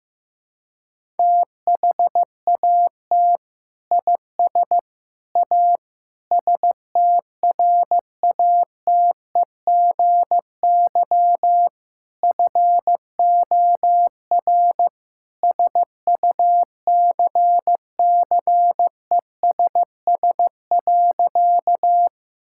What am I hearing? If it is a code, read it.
Morse code, heard as a keyed tone, about 15 wpm, 700 Hz. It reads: THAT IS A STRATEGY FOR SUCCESS.
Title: Sentences from Top 1000 Words - Review Part 5 - 15wpm